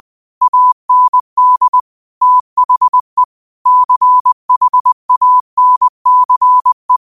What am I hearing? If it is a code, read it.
Morse code, heard as a keyed tone, about 20 wpm, 1000 Hz. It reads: AND THE CHANCE